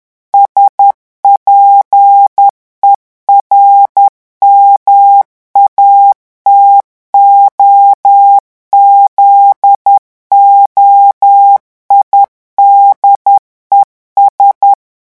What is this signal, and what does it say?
Morse code, heard as a keyed tone, about 10 wpm, 800 Hz. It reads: SPERMATOZOIDES
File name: revision9 citations humoristiques vitesse 10 mots minute